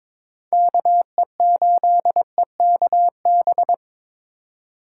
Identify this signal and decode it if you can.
Morse code, heard as a keyed tone, about 26 wpm, 700 Hz. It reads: KE8EKB
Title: Speed Racing - USA 2x3 Call Signs 22wpm